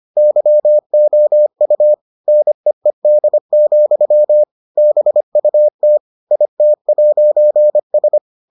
Morse code, heard as a keyed tone, about 25 wpm, 600 Hz. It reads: YOU NEED, BUT IT'S